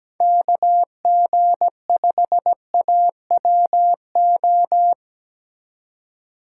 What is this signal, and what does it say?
Morse code, heard as a keyed tone, about 17 wpm, 700 Hz. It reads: KG5AWO
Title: USA Call Signs - 6 Characters - 2x Character Spacing - 17wpm